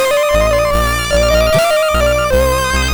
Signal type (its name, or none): none